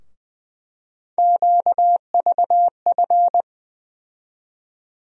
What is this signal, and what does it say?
Morse code, heard as a keyed tone, about 20 wpm, 700 Hz. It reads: QVF
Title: USA Call Sign Postfixes - 3 Letters 20wpm